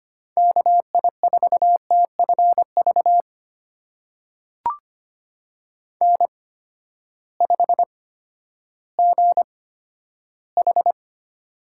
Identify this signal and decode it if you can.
Morse code, heard as a keyed tone, about 25 wpm, spaced as for 12 wpm, 700 Hz. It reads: KI4TFV N5GH
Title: USA Call Signs - 8x Character Spacing - 25wpm